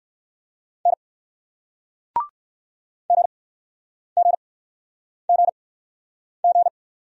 Morse code, heard as a keyed tone, about 59 wpm, spaced as for 9 wpm, 700 Hz. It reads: N GGGG